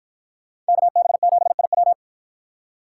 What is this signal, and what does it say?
Morse code, heard as a keyed tone, about 51 wpm, 700 Hz. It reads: KB7IW